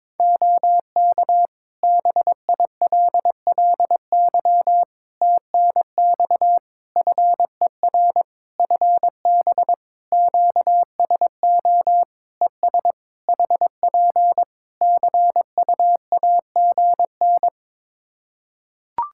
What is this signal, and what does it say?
Morse code, heard as a keyed tone, about 22 wpm, 700 Hz. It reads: OK BILLY TNX FER FB QSO ES HP CUAGN
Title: Practice QSO - Exchange 4 - Ending - 22wpm